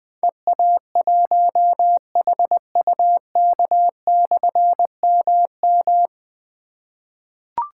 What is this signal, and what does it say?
Morse code, heard as a keyed tone, about 20 wpm, 700 Hz. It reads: EA1HUK/MM
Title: Difficult Call Signs 20wpm